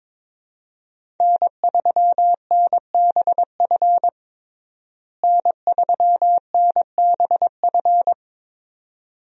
Morse code, heard as a keyed tone, about 22 wpm, 700 Hz. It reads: N3NBF N3NBF
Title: USA Call Signs - 5 Characters - Repeated 2x - 22wpm